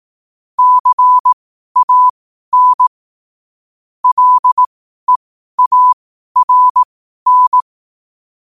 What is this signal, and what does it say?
Morse code, heard as a keyed tone, about 18 wpm, spaced as for 12 wpm, 1000 Hz. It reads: CAN LEARN